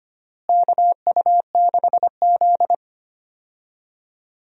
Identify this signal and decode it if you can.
Morse code, heard as a keyed tone, about 25 wpm, 700 Hz. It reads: KU6Z
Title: USA Call Signs - 4 Characters - Repeated 3x - 25wpm